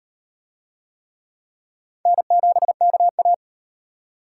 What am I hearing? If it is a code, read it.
Morse code, heard as a keyed tone, about 34 wpm, 700 Hz. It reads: N7KA